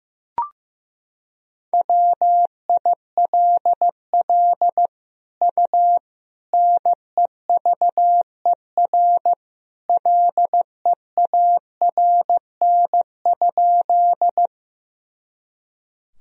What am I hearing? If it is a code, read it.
Morse code, heard as a keyed tone, about 15 wpm, 700 Hz. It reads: WILL U NEVER LEARN?
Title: Sentences from Top 300 Words and Abbreviations 15wpm